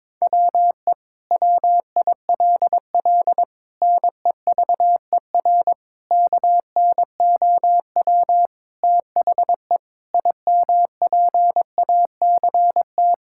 Morse code, heard as a keyed tone, about 22 wpm, 700 Hz. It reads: WE WILL NEVER KNOW THE IMPACT
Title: Sentences from Top 1000 Words - Review Part 6 - 22wpm